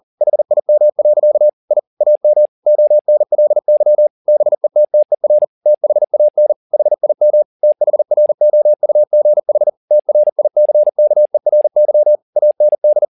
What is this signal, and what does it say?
Morse code, heard as a keyed tone, about 40 wpm, 600 Hz. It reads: HIM. I AM ONLY BETTER THAN HIM THROUGH TRICKERY AND
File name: the old man and the sea 2 40wpm